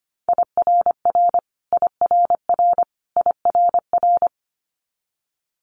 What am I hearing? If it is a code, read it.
Morse code, heard as a keyed tone, about 25 wpm, 700 Hz. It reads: IRR IRR IRR
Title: Multi-letter Phonemes Repeated 3x 25wpm